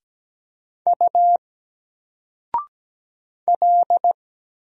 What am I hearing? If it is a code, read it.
Morse code, heard as a keyed tone, about 17 wpm, 700 Hz. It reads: U L
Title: Single Letter-Number 17wpm